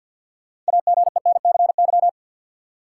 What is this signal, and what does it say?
Morse code, heard as a keyed tone, about 52 wpm, 700 Hz. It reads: AGENCY
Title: Sentences from Top 1300 Words - New Word Drill Alt - 50wpm